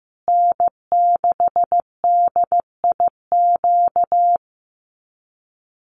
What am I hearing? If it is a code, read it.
Morse code, heard as a keyed tone, about 15 wpm, 700 Hz. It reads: N6DIQ
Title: USA Call Signs Repeated 3x 15wpm